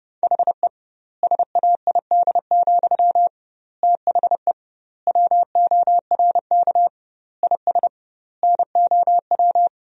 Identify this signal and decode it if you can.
Morse code, heard as a keyed tone, about 30 wpm, 700 Hz. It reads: HE SAID, THE WORK IS NOW
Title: 2x Word-Spacing - Sentences from Top 400 Words - 30wpm